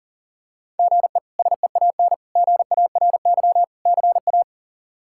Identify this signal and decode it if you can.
Morse code, heard as a keyed tone, about 40 wpm, 700 Hz. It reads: GE SEAN GARY CA
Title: K1USN SST QSO Practice - 40wpm